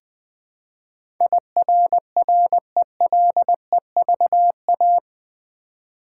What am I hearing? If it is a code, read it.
Morse code, heard as a keyed tone, about 20 wpm, 700 Hz. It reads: IRRELEVA
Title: Build A Word - 4 Letters At A Time - 10-Letter Words - 20wpm